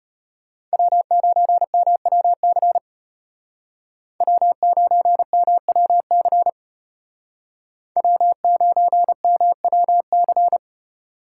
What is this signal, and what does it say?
Morse code, heard as a keyed tone, about 34 wpm, 700 Hz. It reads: W9MWC W9MWC W9MWC